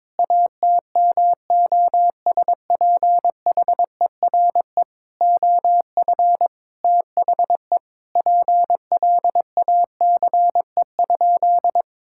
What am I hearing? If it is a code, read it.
Morse code, heard as a keyed tone, about 21 wpm, 700 Hz. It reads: ATMOSPHERE OF THE PLACE?
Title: Sentences from Top 1600 Words 22wpm